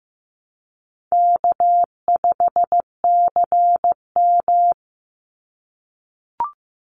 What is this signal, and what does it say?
Morse code, heard as a keyed tone, about 15 wpm, 700 Hz. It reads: K5CM
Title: Easy Call Signs - 15wpm